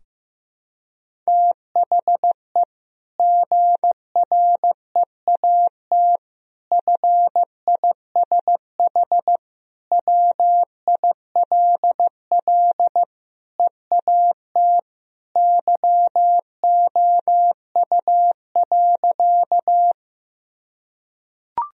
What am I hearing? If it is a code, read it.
Morse code, heard as a keyed tone, about 15 wpm, 700 Hz. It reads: THE GREAT FISH WILL EAT YOU.
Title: Sentences from Top 1000 Words - Review Part 1 - 15wpm